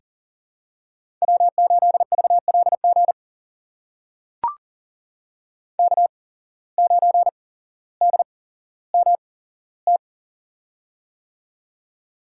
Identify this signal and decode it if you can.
Morse code, heard as a keyed tone, about 41 wpm, spaced as for 17 wpm, 700 Hz. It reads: W8VLG K9DMT
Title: USA Call Signs - 5 Characters - 8x Character Spacing - 40wpm